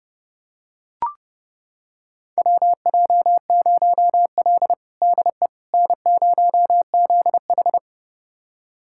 Morse code, heard as a keyed tone, about 30 wpm, 700 Hz. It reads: WJ0L DE N0ZH
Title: USA Callsign DE Callsign - 4 Characters - 30wpm